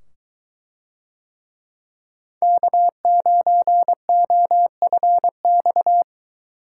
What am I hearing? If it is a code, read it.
Morse code, heard as a keyed tone, about 21 wpm, 700 Hz. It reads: K9OFX